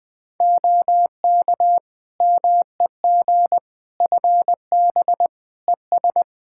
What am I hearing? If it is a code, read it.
Morse code, heard as a keyed tone, about 20 wpm, 700 Hz. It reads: OK MEG FB ES
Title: Practice QSO - Exchange 2 - Info - 20wpm